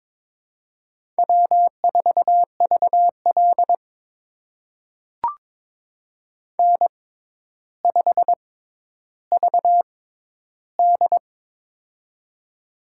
Morse code, heard as a keyed tone, about 22 wpm, spaced as for 11 wpm, 700 Hz. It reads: W4VL N5VD
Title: USA Call Signs - 4 Characters - 6x Character Spacing - 22wpm